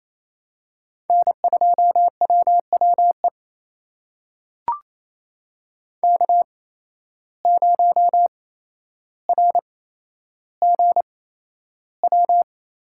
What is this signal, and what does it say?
Morse code, heard as a keyed tone, about 28 wpm, spaced as for 12 wpm, 700 Hz. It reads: N2WWE K0RGW
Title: USA Call Signs - 5 Characters - 8x Character Spacing - 28wpm